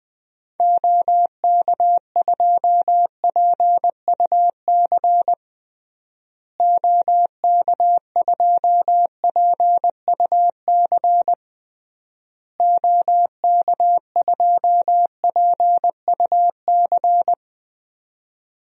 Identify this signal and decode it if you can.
Morse code, heard as a keyed tone, about 20 wpm, 700 Hz. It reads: OK2PUC OK2PUC OK2PUC
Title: DXCC-100 Call Signs - 6 Characters - Repeated 3x - 20wpm